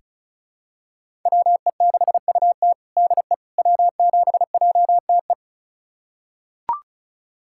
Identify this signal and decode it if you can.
Morse code, heard as a keyed tone, about 36 wpm, 700 Hz. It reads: WE6UT DE W7JTE
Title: USA Callsign DE Callsign - 5 Characters - 35wpm